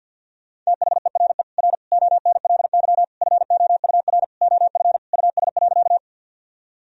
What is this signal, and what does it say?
Morse code, heard as a keyed tone, about 46 wpm, 700 Hz. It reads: THERE R ONLY FOUR OF US.
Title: Sentences from Top 300 Words and Abbreviations 50wpm